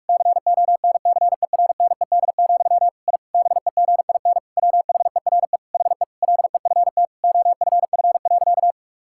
Morse code, heard as a keyed tone, about 48 wpm, 700 Hz. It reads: CONCERNED, I BEGIN WHERE HE LEFT OFF.
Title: Sentences from Top 1000 Words - Review Part 6 - 45wpm